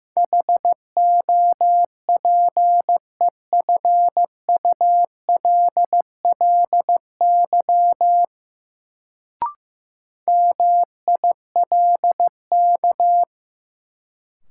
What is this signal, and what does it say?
Morse code, heard as a keyed tone, about 15 wpm, 700 Hz. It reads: HOPEFULLY MILK
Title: Sentences from Top 1500 Words - New Word Drill - 15wpm